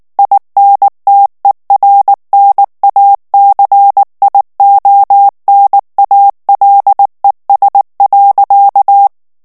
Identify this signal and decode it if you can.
Morse code, heard as a keyed tone, about 19 wpm, 800 Hz. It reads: INTERNACIONALES.
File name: morse mp3 / ESPAGNOL TEXTE EN CLAIR 2 VITESSE 18 MOTS MINUTES